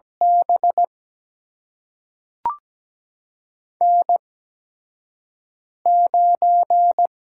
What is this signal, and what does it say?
Morse code, heard as a keyed tone, about 17 wpm, spaced as for 5 wpm, 700 Hz. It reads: B N9
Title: USA Call Signs - 4 Characters - 8x Character Spacing - 17wpm